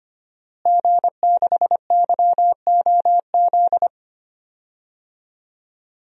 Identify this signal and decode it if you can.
Morse code, heard as a keyed tone, about 25 wpm, 700 Hz. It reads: G6YOZ
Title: DXCC-100 Call Signs - 5 Characters - Repeated 2x - 25wpm